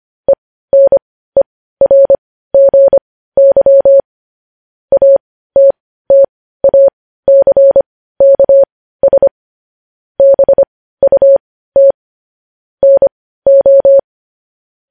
Morse code, heard as a keyed tone, about 25 wpm, spaced as for 15 wpm, 550 Hz. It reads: ENERGY ATTACKS BUT NO
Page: news headlines in morse code at 15 wpm